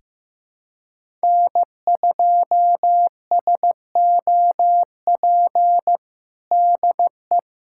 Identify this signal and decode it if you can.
Morse code, heard as a keyed tone, about 15 wpm, 700 Hz. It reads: N2SOP DE
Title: USA Callsign DE Callsign - 5 Characters - 15wpm